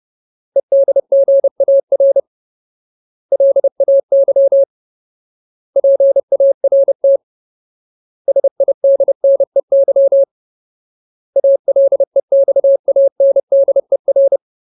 Morse code, heard as a keyed tone, about 30 wpm, 550 Hz. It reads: EDGAR LAY PART SIDNEY ALEXANDER
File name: Ultimate Word List / 30WPM Ultimate